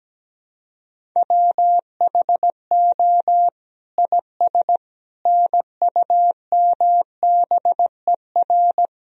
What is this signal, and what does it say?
Morse code, heard as a keyed tone, about 17 wpm, 700 Hz. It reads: WHO IS NUMBER
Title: Sentences from Top 1000 Words - Review Part 1 - 17wpm